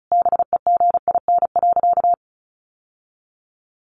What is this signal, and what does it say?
Morse code, heard as a keyed tone, about 34 wpm, 700 Hz. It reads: BEGIN.
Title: Sentences from Top 1200 Words 35wpm